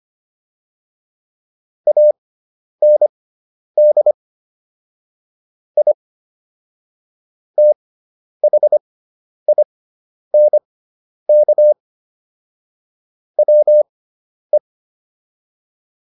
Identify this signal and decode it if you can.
Morse code, heard as a keyed tone, about 25 wpm, spaced as for 10 wpm, 600 Hz. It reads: AND I THINK WE